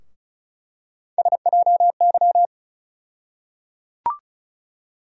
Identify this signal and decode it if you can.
Morse code, heard as a keyed tone, about 36 wpm, 700 Hz. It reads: SJY
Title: USA Call Sign Postfixes - 3 Letters 35wpm